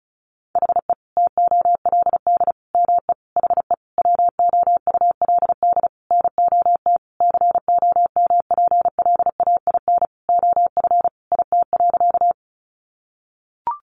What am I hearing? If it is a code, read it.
Morse code, heard as a keyed tone, about 34 wpm, 700 Hz. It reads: HE TOLD ME HE WOULD NOT COMPLAIN OF IT.
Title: Sentences from Top 1300 Words 35wpm